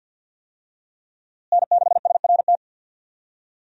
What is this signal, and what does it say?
Morse code, heard as a keyed tone, about 47 wpm, 700 Hz. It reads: N6SRT
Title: USA Call Signs - 5 Characters - Repeated 2x - 50wpm